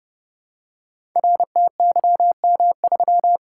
Morse code, heard as a keyed tone, about 29 wpm, 700 Hz. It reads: RTYM3
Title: Code Groups - 5 Letter:Numbers - 30wpm